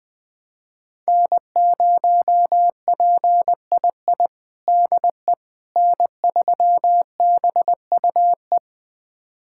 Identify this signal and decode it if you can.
Morse code, heard as a keyed tone, about 20 wpm, 700 Hz. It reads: N0PII DE N3BUE
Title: USA Callsign DE Callsign - 5 Characters - 20wpm